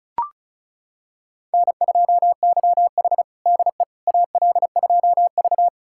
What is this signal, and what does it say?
Morse code, heard as a keyed tone, about 36 wpm, 700 Hz. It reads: N2YH DE AL2V
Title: USA Callsign DE Callsign - 4 Characters - 35wpm